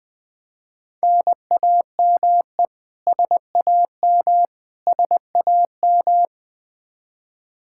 Morse code, heard as a keyed tone, about 21 wpm, 700 Hz. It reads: NAME SAM SAM
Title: Practice QSO - Exchange 1 - Beginning - 20wpm